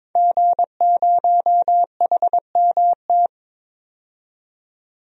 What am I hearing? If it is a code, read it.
Morse code, heard as a keyed tone, about 22 wpm, 700 Hz. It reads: G0HMT
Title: DXCC-100 Call Signs - 5 Characters - 22wpm